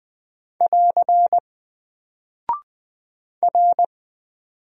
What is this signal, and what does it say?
Morse code, heard as a keyed tone, about 20 wpm, 700 Hz. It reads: <AR> R
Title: Single Character (Letter:Number:Punctuation:Prosign) Alt 20wpm